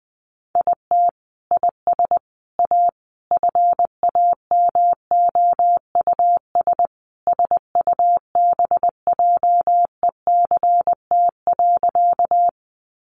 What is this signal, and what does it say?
Morse code, heard as a keyed tone, about 19 wpm, 700 Hz. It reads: IT IS A FAMOUS SUBJECT.
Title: Sentences from Top 1500 Words 20wpm